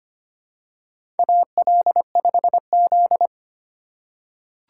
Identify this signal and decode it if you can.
Morse code, heard as a keyed tone, about 25 wpm, 700 Hz. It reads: AL5Z
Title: USA Call Signs - 4 Characters - 4x Character Spacing - 25wpm